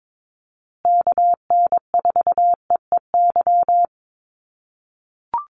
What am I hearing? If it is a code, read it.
Morse code, heard as a keyed tone, about 22 wpm, 690 Hz. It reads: KN4EEY
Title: USA Call Signs Repeated 3x 22wpm